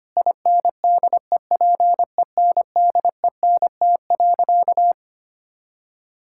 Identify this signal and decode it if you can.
Morse code, heard as a keyed tone, about 25 wpm, 700 Hz. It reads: INDEPENDENT.
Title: Sentences from Top 1000 Words - Review Part 6 - 25wpm